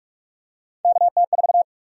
Morse code, heard as a keyed tone, about 42 wpm, 700 Hz. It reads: KT4
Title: USA Call Sign Prefixes - 45wpm